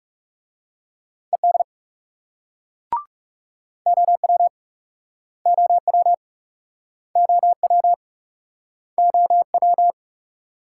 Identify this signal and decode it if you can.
Morse code, heard as a keyed tone, about 37 wpm, 700 Hz. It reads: ED OW OW OW OW
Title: Speed-Racing Multi-letter Phonemes - 30wpm